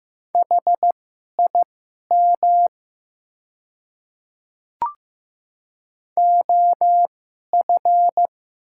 Morse code, heard as a keyed tone, about 15 wpm, spaced as for 11 wpm, 700 Hz. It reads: HIM OF
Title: 2x Character Spacing - Top 100 Words - 15wpm